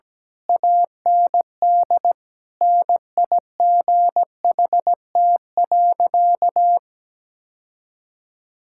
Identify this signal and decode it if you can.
Morse code, heard as a keyed tone, about 17 wpm, 700 Hz. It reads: AND NIGHT.
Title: Sentences from Top 1000 Words - Review Part 6 - 17wpm